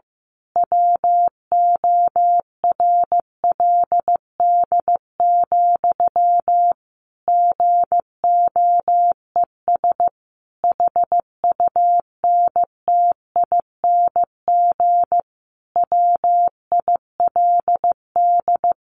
Morse code, heard as a keyed tone, about 15 wpm, 700 Hz. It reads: WORLD, GOES HUNTING WILD
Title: A Sound Of Thunder 15wpm